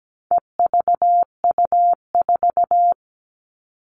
Morse code, heard as a keyed tone, about 17 wpm, 700 Hz. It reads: EVU4
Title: All Cut Numbers - 4-Digit 17wpm